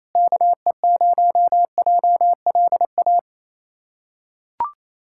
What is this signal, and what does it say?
Morse code, heard as a keyed tone, about 27 wpm, 700 Hz. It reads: KE0JLA
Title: USA Callsign DE Callsign - 6 Characters - 28wpm